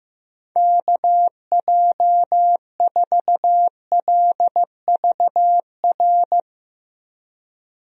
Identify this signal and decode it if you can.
Morse code, heard as a keyed tone, about 15 wpm, 700 Hz. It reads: KJ4LVR
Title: USA Call Signs - 6 Characters - 15wpm